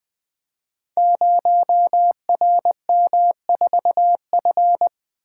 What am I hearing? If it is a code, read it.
Morse code, heard as a keyed tone, about 20 wpm, 700 Hz. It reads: 0RM4F
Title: Code Groups - 5 Letter:Numbers - 20wpm